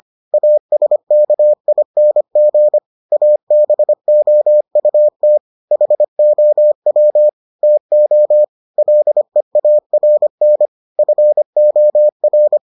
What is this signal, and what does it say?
Morse code, heard as a keyed tone, about 25 wpm, 600 Hz. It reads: ASKING ABOUT HOW TO LEARN FOR